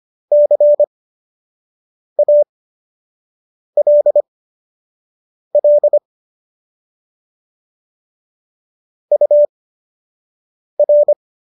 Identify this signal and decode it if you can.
Morse code, heard as a keyed tone, about 25 wpm, spaced as for 6 wpm, 590 Hz. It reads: CALL UR